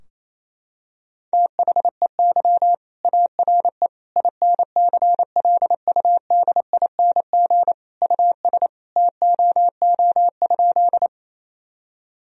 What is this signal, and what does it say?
Morse code, heard as a keyed tone, about 28 wpm, 700 Hz. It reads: THEY ARE INCLUDING US TOO?
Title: Sentences from Top 1000 Words - Review Part 4 - 28wpm